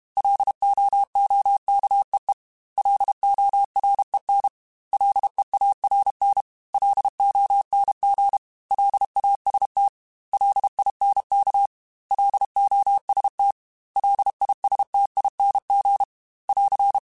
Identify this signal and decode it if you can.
Morse code, heard as a keyed tone, about 32 wpm, 800 Hz. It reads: LOOKEE LOREN LEARN LONG LAST LINK LOST LISTING <AR>
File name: L Runs / L Runs mp3s / L Runs @30